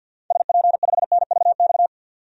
Morse code, heard as a keyed tone, about 48 wpm, 700 Hz. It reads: SP5NVX
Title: DXCC-100 Call Signs - 6 Characters - 50wpm